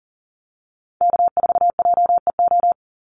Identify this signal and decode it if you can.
Morse code, heard as a keyed tone, about 40 wpm, 700 Hz. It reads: K4JEO
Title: USA Call Signs 40wpm